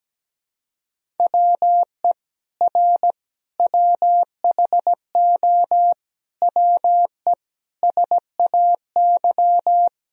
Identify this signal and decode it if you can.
Morse code, heard as a keyed tone, about 17 wpm, 700 Hz. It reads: WE R WHO WE SAY
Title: Sentences from Top 200 Words and Abbreviations 17wpm